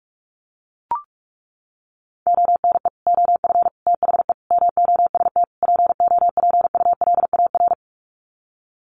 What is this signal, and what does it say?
Morse code, heard as a keyed tone, about 44 wpm, 700 Hz. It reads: ONE OF THE MOST POPULAR
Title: Sets of 5 Words 45wpm